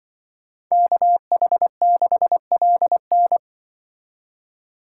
Morse code, heard as a keyed tone, about 25 wpm, 700 Hz. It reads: KH6LN